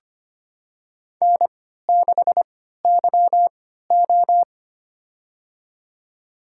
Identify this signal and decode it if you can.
Morse code, heard as a keyed tone, about 25 wpm, spaced as for 14 wpm, 700 Hz. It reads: N6YO
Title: USA Call Signs - 4 Characters - 3x Character Spacing - 25wpm